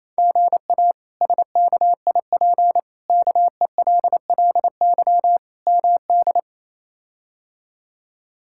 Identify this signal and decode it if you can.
Morse code, heard as a keyed tone, about 28 wpm, 700 Hz. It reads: GA SKIP KELLY MD
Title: K1USN SST QSO Practice - 28wpm